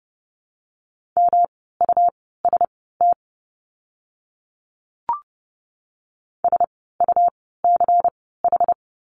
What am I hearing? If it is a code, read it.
Morse code, heard as a keyed tone, about 30 wpm, spaced as for 17 wpm, 700 Hz. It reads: MUST SUCH
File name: course / 30:10wpm / 237 Lesson 38f - Letter q - Words 30wpm